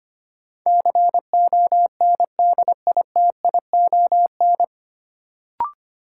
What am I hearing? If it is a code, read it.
Morse code, heard as a keyed tone, about 25 wpm, 700 Hz. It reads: CONDITION
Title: Sentences from Top 1000 Words - New Word Drill - 25wpm